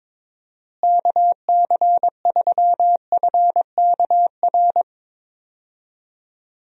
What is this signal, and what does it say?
Morse code, heard as a keyed tone, about 22 wpm, 700 Hz. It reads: KC3FKR
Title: USA Call Signs - 6 Characters - Repeated 2x - 22wpm